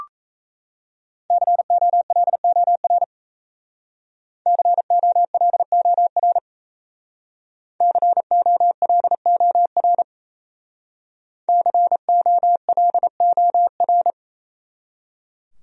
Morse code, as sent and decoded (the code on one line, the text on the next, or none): -.-. --- .-.. --- .-. / -.-. --- .-.. --- .-. / -.-. --- .-.. --- .-. / -.-. --- .-.. --- .-.
COLOR COLOR COLOR COLOR